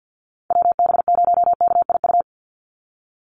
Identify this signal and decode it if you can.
Morse code, heard as a keyed tone, about 50 wpm, 700 Hz. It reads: WB0KIU
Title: NA SOTA Chaser Callsigns 50wpm